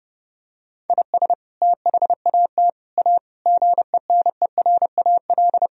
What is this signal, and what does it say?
Morse code, heard as a keyed tone, about 30 wpm, 700 Hz. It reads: IS THAT A GENERAL
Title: Sentences from Top 1000 Words - Review Part 3 - 30wpm